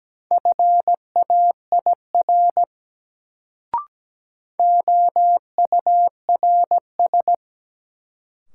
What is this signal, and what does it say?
Morse code, heard as a keyed tone, about 17 wpm, 700 Hz. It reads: FAIR OURS
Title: Sentences from Top 1000 Words - Review Part 5 - New Word Drill - 17wpm